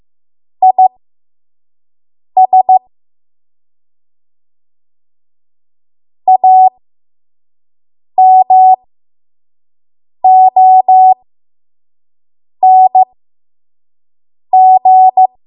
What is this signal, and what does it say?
Morse code, heard as a keyed tone, about 15 wpm, spaced as for 5 wpm, 750 Hz. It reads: IS AMONG
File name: Morse / 05WPM